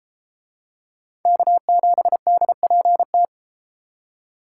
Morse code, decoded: K7DPT

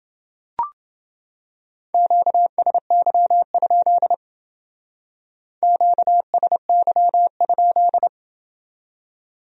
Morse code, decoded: QSY? QSY?

